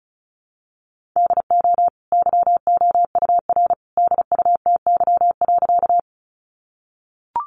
Morse code, heard as a keyed tone, about 34 wpm, 700 Hz. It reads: DO YOUR DUTY.
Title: Sentences from Top 1500 Words 35wpm